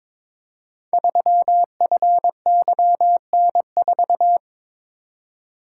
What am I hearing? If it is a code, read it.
Morse code, heard as a keyed tone, about 22 wpm, 700 Hz. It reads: 3FYN4
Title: Code Groups - 5 Letter:Numbers - 22wpm